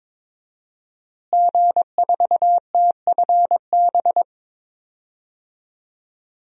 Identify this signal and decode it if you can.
Morse code, heard as a keyed tone, about 21 wpm, 700 Hz. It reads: G4TFB
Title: DXCC-100 Call Signs - 5 Characters - Repeated 3x - 22wpm